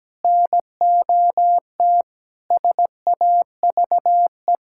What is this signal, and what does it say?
Morse code, heard as a keyed tone, about 17 wpm, 700 Hz. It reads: NOT SAVE
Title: Sentences from Top 1600 Words 17wpm